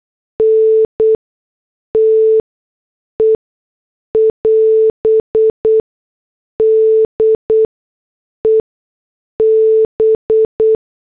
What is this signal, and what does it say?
Morse code, heard as a keyed tone, about 8 wpm, spaced as for 6 wpm, 440 Hz. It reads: NTE <AS> DE B
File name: morse / headlines